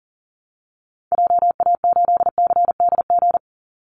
Morse code, heard as a keyed tone, about 40 wpm, 700 Hz. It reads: JA8CDG
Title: Call Signs Repeated 3x - 40wpm